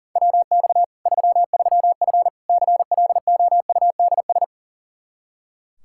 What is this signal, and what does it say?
Morse code, heard as a keyed tone, about 40 wpm, 700 Hz. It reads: WX 33F CLOUDS
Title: Practice QSO - Exchange 2 - Info - 40wpm